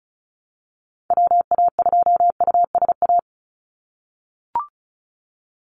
Morse code, heard as a keyed tone, about 34 wpm, 700 Hz. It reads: WA2USA
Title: NA SOTA Chaser Callsigns 35wpm